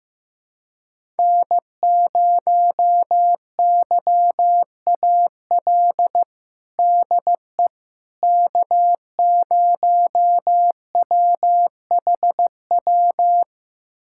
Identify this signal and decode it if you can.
Morse code, heard as a keyed tone, about 15 wpm, 700 Hz. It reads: N0YAL DE K0WHW
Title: USA Callsign DE Callsign - 5 Characters - 15wpm